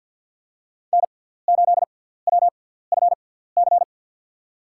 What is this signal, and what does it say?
Morse code, heard as a keyed tone, about 52 wpm, spaced as for 17 wpm, 700 Hz. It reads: N8WFC
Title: USA Call Signs - 5 Characters - 6x Character Spacing - 50wpm